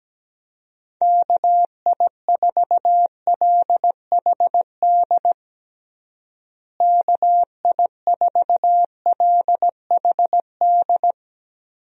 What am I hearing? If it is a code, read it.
Morse code, heard as a keyed tone, about 17 wpm, 700 Hz. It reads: KI4LHD KI4LHD
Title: USA Call Signs - 6 Characters - Repeated 2x - 17wpm